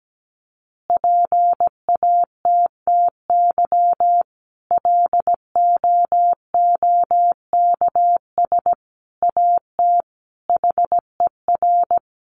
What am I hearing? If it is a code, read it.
Morse code, heard as a keyed tone, about 17 wpm, 700 Hz. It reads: PATTY LOOKS AT HER
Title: Additional Easy Phrases 17wpm